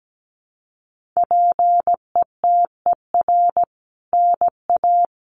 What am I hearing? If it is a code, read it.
Morse code, heard as a keyed tone, about 17 wpm, 700 Hz. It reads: PETER NA
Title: CWT Exchange 17wpm